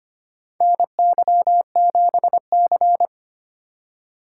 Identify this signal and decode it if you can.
Morse code, heard as a keyed tone, about 25 wpm, 700 Hz. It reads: NY7C
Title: USA Call Signs - 2x Character Spacing - 25wpm